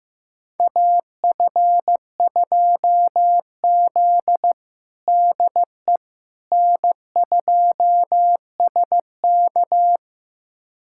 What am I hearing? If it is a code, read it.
Morse code, heard as a keyed tone, about 15 wpm, 700 Hz. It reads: AF2Z DE N2SK
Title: USA Callsign DE Callsign - 4 Characters - 15wpm